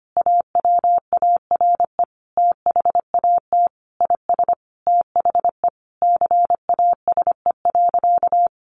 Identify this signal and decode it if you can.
Morse code, heard as a keyed tone, about 25 wpm, 700 Hz. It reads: AWARE THAT IS THE CASE.
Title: Sentences from Top 1200 Words 25wpm